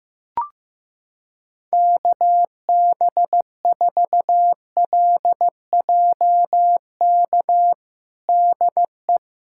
Morse code, heard as a keyed tone, about 16 wpm, 700 Hz. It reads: KB4LJK DE
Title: USA Callsign DE Callsign - 6 Characters - 15wpm